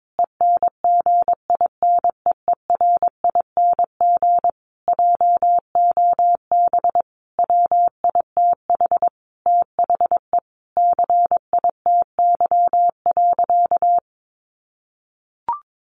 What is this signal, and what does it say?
Morse code, heard as a keyed tone, about 22 wpm, 700 Hz. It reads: ENGINEERING JOB WITH THE CITY.
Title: Sentences from Top 1500 Words 22wpm